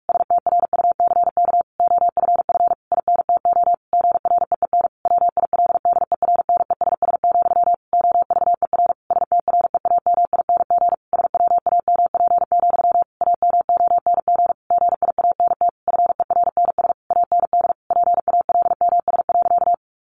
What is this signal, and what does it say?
Morse code, heard as a keyed tone, about 44 wpm, 700 Hz. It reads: STRUCK OFF INTO GREEN WILDERNESS, OVER STREAMING SWAMP, AMONG GIANT FERNS AND PALMS.
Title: A Sound Of Thunder 45wpm